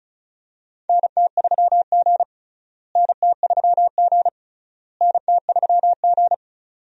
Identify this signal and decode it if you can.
Morse code, heard as a keyed tone, about 36 wpm, 700 Hz. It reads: NT3G NT3G NT3G
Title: USA Call Signs - 4 Characters - Repeated 3x - 35wpm